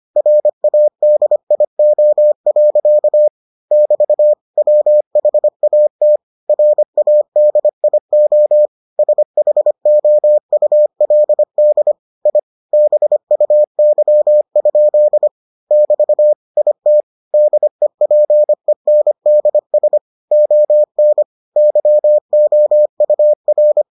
Morse code, heard as a keyed tone, about 25 wpm, 600 Hz. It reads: RADIO. <BT> WHAT RADIO SHOULD I BUY? <BT> IT DEPENDS ON YOUR